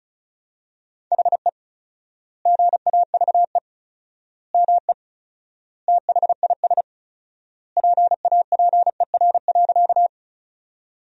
Morse code, heard as a keyed tone, about 36 wpm, 700 Hz. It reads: HE GAVE ME THIS PAPER.
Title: 4x Word-Spacing - Sentences from Top 500 Words - 35wpm